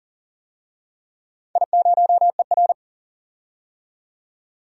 Morse code, heard as a keyed tone, about 41 wpm, 700 Hz. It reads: I0ER
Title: DXCC-100 Call Signs - 4 Characters - 40wpm